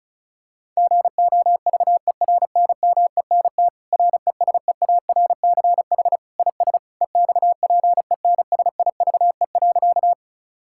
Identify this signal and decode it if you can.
Morse code, heard as a keyed tone, about 36 wpm, 700 Hz. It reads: GOVERNMENT RESEARCH IS EXPENSIVE.
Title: Sentences from Top 1000 Words - Review Part 6 - 35wpm